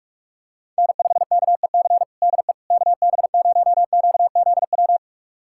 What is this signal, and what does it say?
Morse code, heard as a keyed tone, about 41 wpm, 700 Hz. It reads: N5KEC DE KB0QZW
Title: USA Callsign DE Callsign 45wpm